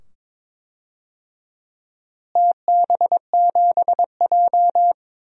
Morse code, heard as a keyed tone, about 21 wpm, 700 Hz. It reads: TB7J
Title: DXCC-100 Call Signs - 4 Characters - Repeated 2x - 22wpm